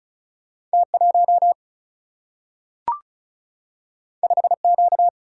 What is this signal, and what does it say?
Morse code, heard as a keyed tone, about 36 wpm, 700 Hz. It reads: T 1 5 Q
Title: Single-Letter Single-Number 35wpm